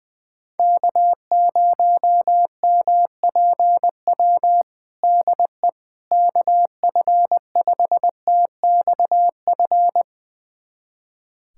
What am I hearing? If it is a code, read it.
Morse code, heard as a keyed tone, about 20 wpm, 700 Hz. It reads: K0MPW DE KF5TXF
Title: USA Callsign DE Callsign 20wpm